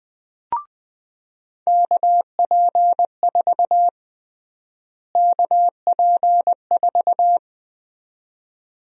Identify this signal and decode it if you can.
Morse code, heard as a keyed tone, about 21 wpm, 700 Hz. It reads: KP4 KP4
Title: USA Call Sign Prefixes - Repeated 2x - 20wpm